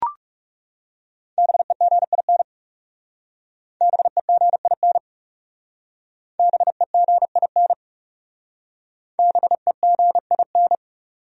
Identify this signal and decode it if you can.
Morse code, heard as a keyed tone, about 37 wpm, 700 Hz. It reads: BEGIN BEGIN BEGIN BEGIN
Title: Speed-Racing Top 300 Words 30wpm